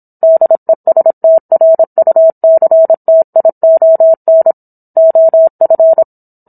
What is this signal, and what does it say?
Morse code, heard as a keyed tone, about 27 wpm, 650 Hz. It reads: DESTRUCTION OF